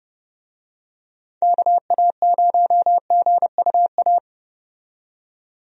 Morse code, heard as a keyed tone, about 30 wpm, 700 Hz. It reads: KA0GUA
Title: USA Call Signs - 6 Characters - 30wpm